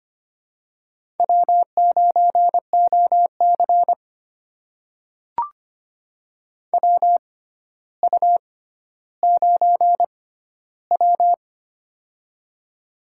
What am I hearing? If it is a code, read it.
Morse code, heard as a keyed tone, about 25 wpm, spaced as for 13 wpm, 700 Hz. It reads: W9OC WU9W
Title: USA Call Signs - 4 Characters - 6x Character Spacing - 25wpm